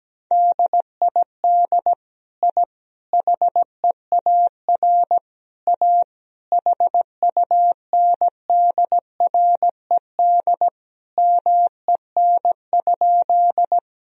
Morse code, heard as a keyed tone, about 17 wpm, 700 Hz. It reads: DID I HEAR A HUNDRED MEN?
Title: Sentences from Top 1000 Words - Review Part 2 - 17wpm